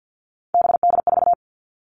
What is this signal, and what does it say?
Morse code, heard as a keyed tone, about 50 wpm, 690 Hz. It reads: BD4
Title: Callsign Prefixes for Top 100 DXCC - 50wpm